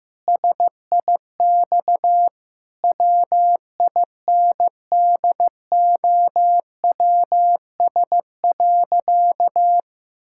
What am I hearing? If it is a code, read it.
Morse code, heard as a keyed tone, about 15 wpm, 700 Hz. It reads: SIX WINDOWS.
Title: Pangram Challenge v2 - 15wpm